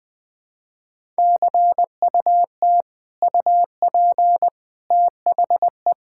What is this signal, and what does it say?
Morse code, heard as a keyed tone, about 20 wpm, 700 Hz. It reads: CUT UP THE
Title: Sentences from Top 1000 Words - Review Part 5 - 20wpm